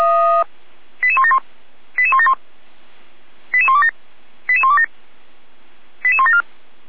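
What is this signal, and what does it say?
Five-tone selective call (ZVEI), heard as quick runs of five tones, tones of about 70 ms. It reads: 88171 88171 88127 88127 88174
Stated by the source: ZVEI five-tone call received via SDR.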